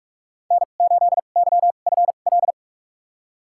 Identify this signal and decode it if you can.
Morse code, heard as a keyed tone, about 46 wpm, spaced as for 32 wpm, 700 Hz. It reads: N8YFL
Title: USA Call Signs - 5 Characters - 2x Character Spacing - 45wpm